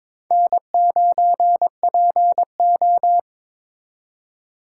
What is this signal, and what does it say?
Morse code, heard as a keyed tone, about 25 wpm, 700 Hz. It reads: N9PO
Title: Speed Racing - USA 1x2 Call Signs 22wpm